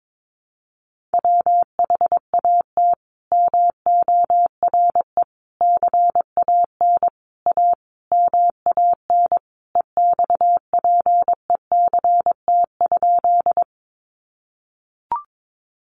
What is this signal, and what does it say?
Morse code, heard as a keyed tone, about 22 wpm, 700 Hz. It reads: WHAT MORE CAN A MAN EXPECT?
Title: Sentences from Top 1100 Words 22wpm